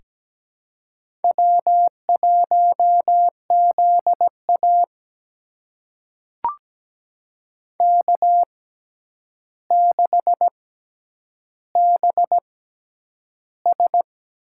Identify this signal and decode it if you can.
Morse code, heard as a keyed tone, about 17 wpm, spaced as for 9 wpm, 700 Hz. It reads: W1ZA K6BS
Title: USA Call Signs - 4 Characters - 6x Character Spacing - 17wpm